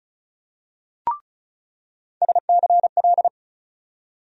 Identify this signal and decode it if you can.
Morse code, heard as a keyed tone, about 36 wpm, 700 Hz. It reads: SCL
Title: USA Call Sign Postfixes - 3 Letters 35wpm